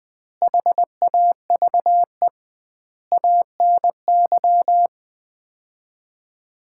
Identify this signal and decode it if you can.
Morse code, heard as a keyed tone, about 21 wpm, 700 Hz. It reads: HAVE ANY
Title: 2x Word-Spacing - Sets of 4 Words - 20wpm